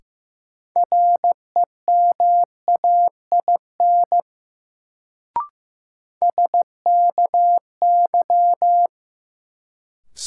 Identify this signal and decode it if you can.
Morse code, heard as a keyed tone, about 15 wpm, 700 Hz. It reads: REMAIN SKY
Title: Sentences from Top 1200 Words - New Word Drill - 15wpm